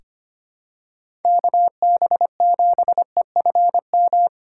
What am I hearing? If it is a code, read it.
Morse code, heard as a keyed tone, about 25 wpm, 700 Hz. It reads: KB7EFM
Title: USA Call Signs - 6 Characters - 2x Character Spacing - 25wpm